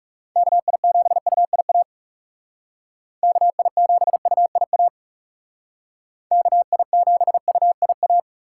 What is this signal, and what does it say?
Morse code, heard as a keyed tone, about 35 wpm, 700 Hz. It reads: KI7UIA KI7UIA KI7UIA
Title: Speed Racing - USA Call Signs 30wpm